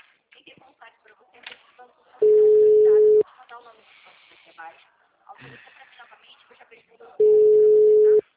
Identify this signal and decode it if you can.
Ringback tone (Europe, 425 Hz), with speech.
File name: out- -2005- - - .104